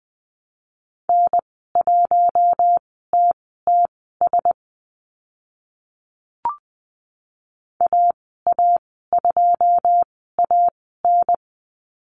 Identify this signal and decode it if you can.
Morse code, heard as a keyed tone, about 20 wpm, spaced as for 14 wpm, 700 Hz. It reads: N1TTS AA2AN
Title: 082 Lesson 14g - Number 2 - Call Signs 20wpm